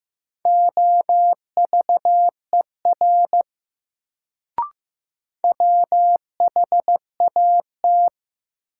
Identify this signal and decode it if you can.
Morse code, heard as a keyed tone, about 15 wpm, 700 Hz. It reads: OVER WHAT
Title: Sentences from Top 100 Words - New Word Drill - 15wpm